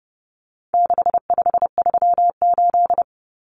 Morse code, heard as a keyed tone, about 30 wpm, 700 Hz. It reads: 6538
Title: Common Cut Numbers - 4-Digit 30wpm